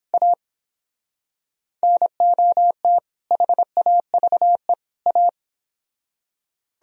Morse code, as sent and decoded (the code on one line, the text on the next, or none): .- / -. --- - / .... .- ...- . / .-
A NOT HAVE A